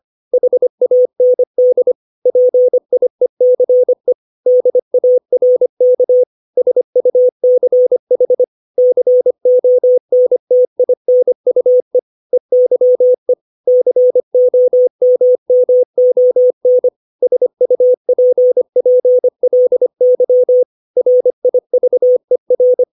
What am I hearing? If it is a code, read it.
Morse code, heard as a keyed tone, about 25 wpm, 500 Hz. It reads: HAND PIECE DARK SUCH CONTINUE EYE COMMON SUPPLY RIVER
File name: 632 words / 632 Words 25WPM